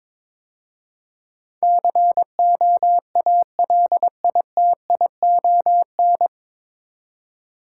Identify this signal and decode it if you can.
Morse code, heard as a keyed tone, about 22 wpm, 700 Hz. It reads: COALITION
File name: 9 Letter Words - First 3 Letters Given - 22wpm